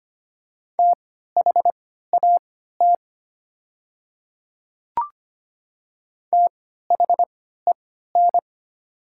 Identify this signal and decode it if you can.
Morse code, heard as a keyed tone, about 25 wpm, spaced as for 14 wpm, 700 Hz. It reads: THAT THEN
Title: 3x Character Spacing - Top 100 Words - 25wpm